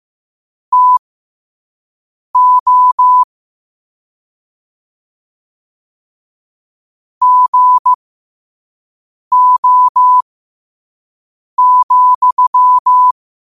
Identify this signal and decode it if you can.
Morse code, heard as a keyed tone, about 15 wpm, spaced as for 5 wpm, 1000 Hz. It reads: TO GO,
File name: qotdfiles / 5wpm